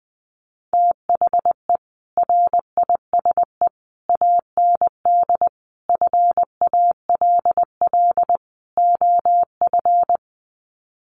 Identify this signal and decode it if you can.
Morse code, heard as a keyed tone, about 20 wpm, 700 Hz. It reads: THE RISE AND FALL OF
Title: Sets of 5 Words Encore2 20wpm